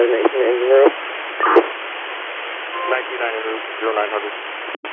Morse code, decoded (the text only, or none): none